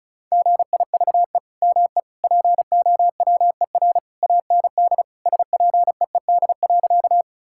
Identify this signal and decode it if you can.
Morse code, heard as a keyed tone, about 36 wpm, 700 Hz. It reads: GIVE ME POWER AND SPEED.
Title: Sentences from Top 1000 Words - Review Part 5 - 35wpm